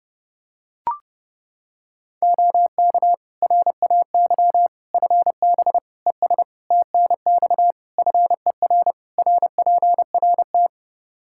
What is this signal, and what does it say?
Morse code, heard as a keyed tone, about 30 wpm, 700 Hz. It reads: OK RAY FB ES TNX FER RPRT
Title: Practice QSO - Exchange 2 - Info - 30wpm